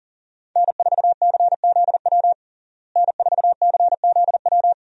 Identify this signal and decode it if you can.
Morse code, heard as a keyed tone, about 37 wpm, 700 Hz. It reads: N4CZW N4CZW
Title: USA Call Signs - 5 Characters - Repeated 2x - 40wpm